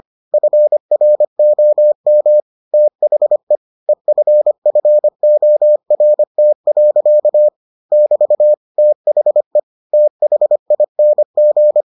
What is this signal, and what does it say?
Morse code, heard as a keyed tone, about 25 wpm, 600 Hz. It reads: FROM THE EFFORT. <BT> THE THING